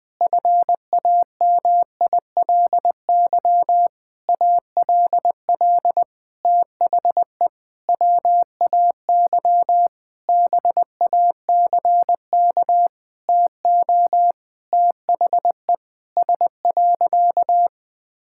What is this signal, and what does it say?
Morse code, heard as a keyed tone, about 21 wpm, 700 Hz. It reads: FAMILY ALL THE WAY BACK TO THE S.
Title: Sentences with 6-Word Phrases - 20wpm